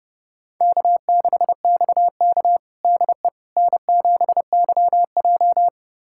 Morse code, heard as a keyed tone, about 32 wpm, 700 Hz. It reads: K6XK DE N7YJ
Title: USA Callsign DE Callsign - 4 Characters - 30wpm